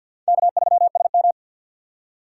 Morse code, heard as a keyed tone, about 49 wpm, 700 Hz. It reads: K3SM
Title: USA Call Signs - 4 Characters - 3x Character Spacing - 50wpm